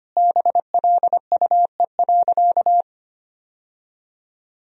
Morse code, heard as a keyed tone, about 25 wpm, 700 Hz. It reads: BLUE.
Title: Sentences from Top 1000 Words - Review Part 3 - 25wpm